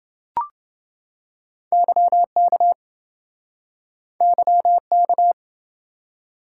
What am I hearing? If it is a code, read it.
Morse code, heard as a keyed tone, about 29 wpm, 700 Hz. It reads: YK YK